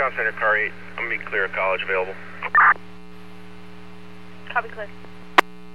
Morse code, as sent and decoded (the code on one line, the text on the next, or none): none